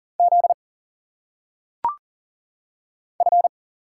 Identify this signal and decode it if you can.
Morse code, heard as a keyed tone, about 40 wpm, 700 Hz. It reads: Z F
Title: Single Letters - Phonetic Pronunciation 40wpm